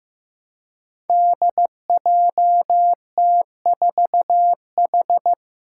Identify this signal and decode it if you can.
Morse code, heard as a keyed tone, about 15 wpm, 700 Hz. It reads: DJT4H
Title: Five Challenging Letter-Number Combinations 15wpm